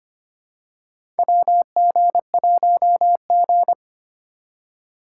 Morse code, heard as a keyed tone, about 25 wpm, 700 Hz. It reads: WG1G
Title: USA Call Signs - 3x Character Spacing - 25wpm